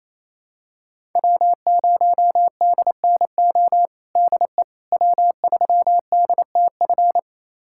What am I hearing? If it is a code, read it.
Morse code, heard as a keyed tone, about 28 wpm, 700 Hz. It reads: W0DNO DE W3DTF
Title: USA Callsign DE Callsign - 5 Characters - 28wpm